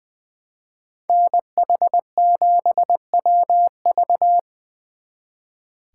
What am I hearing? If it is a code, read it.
Morse code, heard as a keyed tone, about 23 wpm, 700 Hz. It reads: NH7WV
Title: Speed Racing - USA 2x2 Call Signs 20wpm